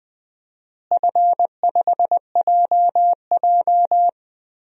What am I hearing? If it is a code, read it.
Morse code, heard as a keyed tone, about 20 wpm, 700 Hz. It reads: F5JJ
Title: DXCC-100 Call Signs - 4 Characters - Repeated 3x - 20wpm